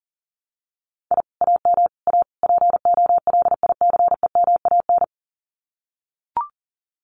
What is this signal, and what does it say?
Morse code, heard as a keyed tone, about 37 wpm, 700 Hz. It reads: I AM A POLICEMAN
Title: Common QSO Phrases Encore 40wpm